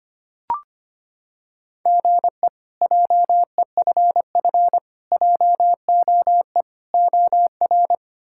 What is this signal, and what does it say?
Morse code, heard as a keyed tone, about 25 wpm, 700 Hz. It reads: GE JEFF JOE OR
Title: K1USN SST QSO Practice - 25wpm